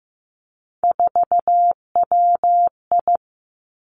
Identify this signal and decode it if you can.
Morse code, heard as a keyed tone, about 15 wpm, 700 Hz. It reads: 4WI
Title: Three Challenging Letter-Number Combinations 15wpm